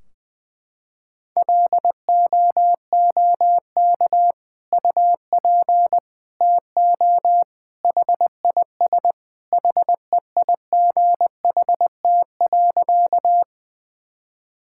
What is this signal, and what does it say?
Morse code, heard as a keyed tone, about 20 wpm, 700 Hz. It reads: LOOK UP TO HIS HEIGHT.
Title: Sentences from Top 1600 Words 20wpm